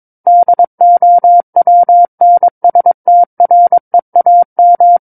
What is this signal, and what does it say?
Morse code, heard as a keyed tone, about 22 wpm, 700 Hz. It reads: DOWNSTREAM
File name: Book One / 13-Chapter 13 - 22 WPM